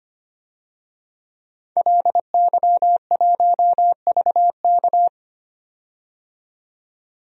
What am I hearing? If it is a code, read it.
Morse code, heard as a keyed tone, about 25 wpm, 700 Hz. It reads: LY1VK
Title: DXCC-100 Call Signs - 5 Characters - 25wpm